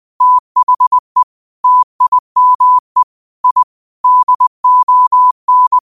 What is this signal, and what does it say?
Morse code, heard as a keyed tone, about 20 wpm, 1000 Hz. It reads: THE TIME I DON